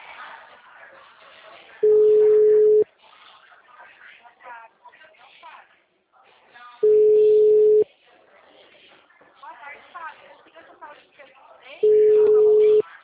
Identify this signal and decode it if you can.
Ringback tone (Europe, 425 Hz), with speech.
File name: out- -2011- - - .88191